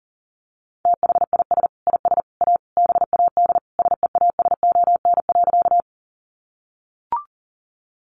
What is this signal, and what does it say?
Morse code, heard as a keyed tone, about 40 wpm, 700 Hz. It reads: THIS IS A BAD SEASON.
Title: Sentences from Top 1400 Words 40wpm